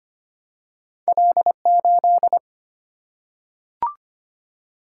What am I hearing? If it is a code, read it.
Morse code, heard as a keyed tone, about 25 wpm, 700 Hz. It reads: L8
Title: Single-Letter Single-Number 25wpm